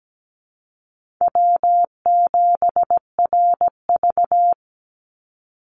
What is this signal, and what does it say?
Morse code, heard as a keyed tone, about 17 wpm, 700 Hz. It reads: W7RV
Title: NA SOTA Chaser Callsigns 17wpm